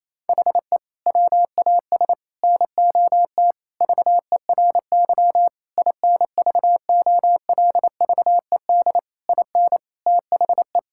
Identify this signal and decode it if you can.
Morse code, heard as a keyed tone, about 28 wpm, 700 Hz. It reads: HE WAS NOT VERY INVOLVED IN THE
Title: Sentences from Top 1000 Words - Review Part 5 - 28wpm